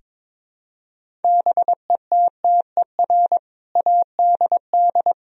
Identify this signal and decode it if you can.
Morse code, heard as a keyed tone, about 22 wpm, 700 Hz. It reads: BETTER ADD
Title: Sentences from Top 1000 Words - Review Part 1 - 22wpm